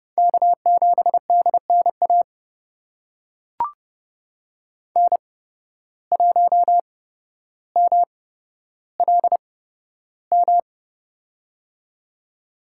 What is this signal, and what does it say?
Morse code, heard as a keyed tone, about 29 wpm, spaced as for 13 wpm, 700 Hz. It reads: K7DNA N1MLM
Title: USA Call Signs - 5 Characters - 8x Character Spacing - 30wpm